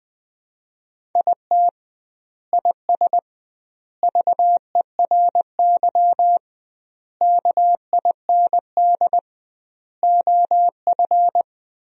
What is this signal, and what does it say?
Morse code, heard as a keyed tone, about 20 wpm, 700 Hz. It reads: IT IS VERY KIND OF
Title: 2x Word-Spacing - Sentences from Top 200 Words - 20wpm